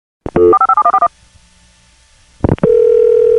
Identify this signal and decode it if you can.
DTMF, heard as telephone touch tones, tones of about 70 ms. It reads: 457141